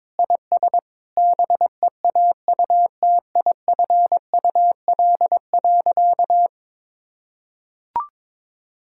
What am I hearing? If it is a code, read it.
Morse code, heard as a keyed tone, about 21 wpm, 700 Hz. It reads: IS BEAUTIFUL.
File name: practice / 22wpm / Sentences with 2-Word Phrases - 22wpm